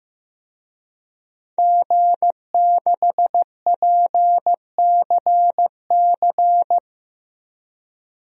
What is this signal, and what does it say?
Morse code, heard as a keyed tone, about 15 wpm, 700 Hz. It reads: G6PCC
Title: DXCC-100 Call Signs - 5 Characters - 15wpm